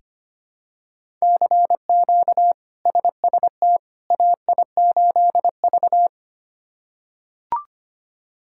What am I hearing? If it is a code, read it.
Morse code, heard as a keyed tone, about 25 wpm, 700 Hz. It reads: CQ SST AI8V
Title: K1USN SST QSO Practice - 25wpm